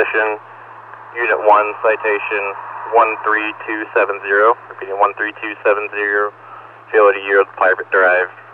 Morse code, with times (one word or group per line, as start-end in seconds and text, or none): none